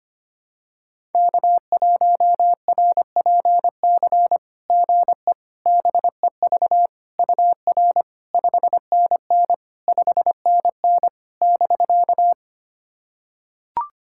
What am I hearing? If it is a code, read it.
Morse code, heard as a keyed tone, about 25 wpm, 690 Hz. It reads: K1RPC GE BEV UR 5NN 5NN <BK>
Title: SOTA QSO Practice 25wpm